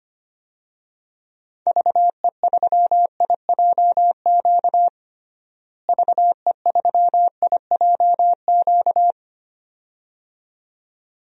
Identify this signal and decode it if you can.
Morse code, heard as a keyed tone, about 25 wpm, 700 Hz. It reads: VE3IJQ VE3IJQ
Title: DXCC-100 Call Signs - 6 Characters - Repeated 2x - 25wpm